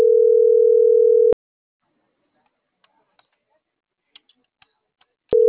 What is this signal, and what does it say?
Ringback tone (North America), 440 + 480 Hz.